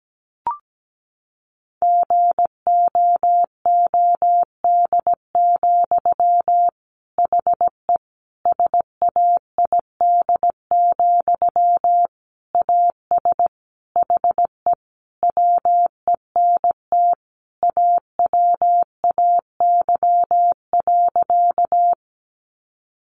Morse code, heard as a keyed tone, about 17 wpm, 700 Hz. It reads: GOOD, HE SAID, AS HE WENT AWAY.
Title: Sentences from Top 600 Words 17wpm